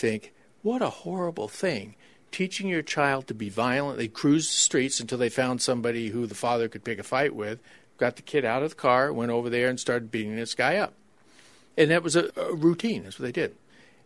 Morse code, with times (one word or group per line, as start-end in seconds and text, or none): none